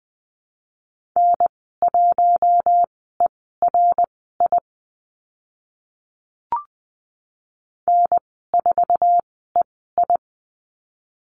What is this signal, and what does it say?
Morse code, heard as a keyed tone, about 20 wpm, spaced as for 14 wpm, 700 Hz. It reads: N1ERI N4EI
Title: 069 Lesson 12g - Letter d - Call Signs 20wpm